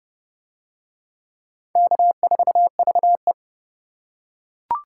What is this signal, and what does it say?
Morse code, heard as a keyed tone, about 29 wpm, 700 Hz. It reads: K4VE